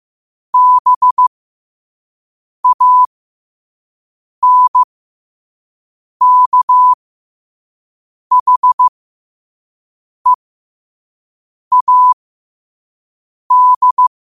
Morse code, heard as a keyed tone, about 15 wpm, spaced as for 5 wpm, 1000 Hz. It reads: BANKHEAD
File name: qotdfiles / 5wpm